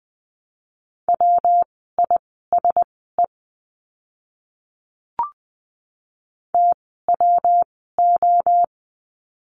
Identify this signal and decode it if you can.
Morse code, heard as a keyed tone, about 20 wpm, spaced as for 15 wpm, 700 Hz. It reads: WISE TWO
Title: 115 Lesson 19f - Letter w - Words 20wpm